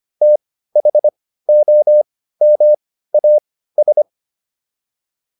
Morse code, heard as a keyed tone, about 25 wpm, spaced as for 15 wpm, 600 Hz. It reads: THOMAS